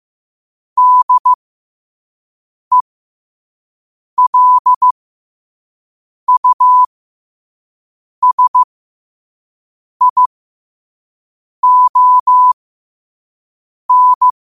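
Morse code, heard as a keyed tone, about 15 wpm, spaced as for 5 wpm, 1000 Hz. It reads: DELUSION
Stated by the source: Morse quote of the day at 5 wpm.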